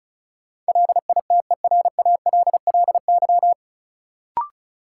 Morse code, heard as a keyed tone, about 36 wpm, 700 Hz. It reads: LITERALLY
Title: Sentences from Top 1400 Words - New Word Drill - 35wpm